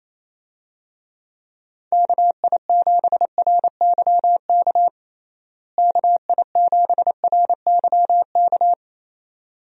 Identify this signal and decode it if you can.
Morse code, heard as a keyed tone, about 28 wpm, 700 Hz. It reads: KI7RYK KI7RYK
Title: USA Call Signs - 6 Characters - Repeated 2x - 28wpm